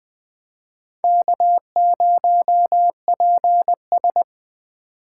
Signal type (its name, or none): Morse code